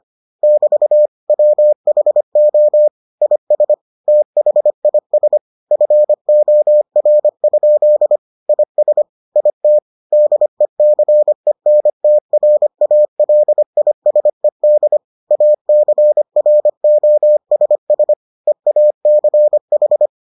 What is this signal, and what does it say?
Morse code, heard as a keyed tone, about 25 wpm, 600 Hz. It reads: <BT> WHO IS THIS FOR? IS IT DECENTRALISED ACROSS EACH